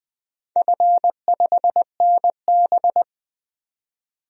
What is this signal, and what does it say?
Morse code, heard as a keyed tone, about 20 wpm, 700 Hz. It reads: F5NB
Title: DXCC-100 Call Signs - 4 Characters - 20wpm